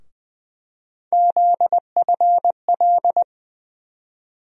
Morse code, heard as a keyed tone, about 20 wpm, 700 Hz. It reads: ZFL
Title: USA Call Sign Postfixes - 3 Letters 20wpm